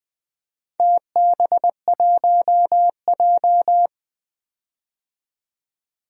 Morse code, heard as a keyed tone, about 20 wpm, 700 Hz. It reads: TB1J
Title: DXCC-100 Call Signs - 4 Characters - 20wpm